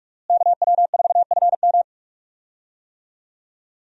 Morse code, heard as a keyed tone, about 41 wpm, 700 Hz. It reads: KW4FM
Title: USA Call Signs - 5 Characters - Repeated 3x - 45wpm